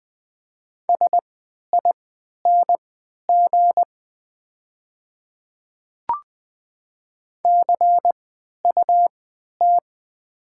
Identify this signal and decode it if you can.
Morse code, heard as a keyed tone, about 20 wpm, spaced as for 11 wpm, 700 Hz. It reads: SING CUT
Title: 3x Character Spacing - Top 500 Words - 20wpm